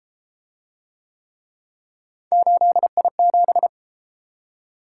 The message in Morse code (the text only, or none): 8I7